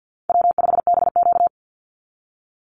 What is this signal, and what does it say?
Morse code, heard as a keyed tone, about 50 wpm, 700 Hz. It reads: W5BQ
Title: Easy Call Signs Repeated 3x - 50wpm